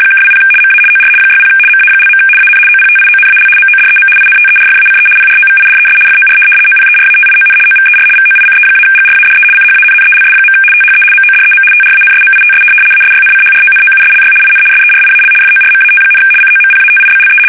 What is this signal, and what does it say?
RTTY, about 75 baud. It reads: (20 MA OR 60 MA). AN OPTIONAL MODEM INTERFACE PROVIDES FOR OPERATION OVER VOICE-GRADE CHANNELS. THE MODEM TRANSMITS ASYNCHRONOUSLY IN SERIAL FORMAT, COMPATIBLE
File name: fsk850-75B / teletype-corp